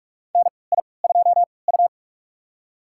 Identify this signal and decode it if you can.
Morse code, heard as a keyed tone, about 44 wpm, spaced as for 25 wpm, 700 Hz. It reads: NI2U
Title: USA Call Signs - 4 Characters - 3x Character Spacing - 45wpm